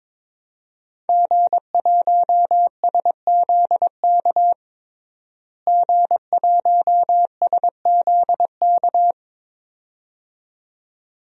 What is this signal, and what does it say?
Morse code, heard as a keyed tone, about 22 wpm, 700 Hz. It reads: G1SZK G1SZK
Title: DXCC-100 Call Signs - 5 Characters - Repeated 2x - 22wpm